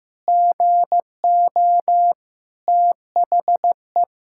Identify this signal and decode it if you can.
Morse code, heard as a keyed tone, about 15 wpm, 700 Hz. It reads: GO THE
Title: Pangram Challenge v2 - 15wpm